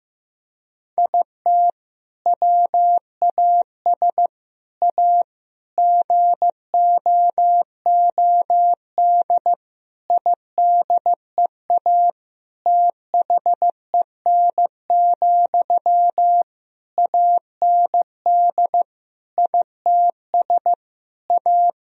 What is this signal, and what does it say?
Morse code, heard as a keyed tone, about 15 wpm, 700 Hz. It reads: IT WAS A GOOD IDEA THEN, AND ITS A
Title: Sentences from Top 1000 Words - Review Part 4 - 15wpm